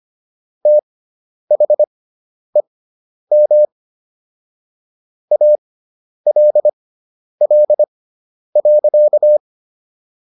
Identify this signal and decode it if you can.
Morse code, heard as a keyed tone, about 25 wpm, spaced as for 10 wpm, 600 Hz. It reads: THEM ALL.